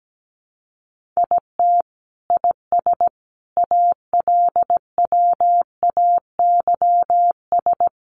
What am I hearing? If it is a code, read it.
Morse code, heard as a keyed tone, about 17 wpm, 700 Hz. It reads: IT IS ALWAYS
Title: Sentences from Top 300 Words 17wpm